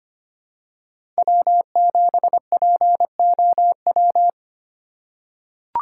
Morse code, heard as a keyed tone, about 25 wpm, 700 Hz. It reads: W7POW
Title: USA Call Signs - 5 Characters - 2x Character Spacing - 25wpm